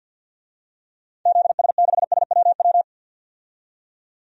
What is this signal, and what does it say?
Morse code, heard as a keyed tone, about 52 wpm, 700 Hz. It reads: ZS6SWW
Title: DXCC-100 Call Signs - 6 Characters - 50wpm